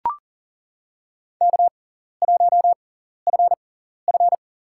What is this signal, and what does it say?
Morse code, heard as a keyed tone, about 40 wpm, spaced as for 14 wpm, 700 Hz. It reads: K1FF
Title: USA Call Signs - 4 Characters - 6x Character Spacing - 40wpm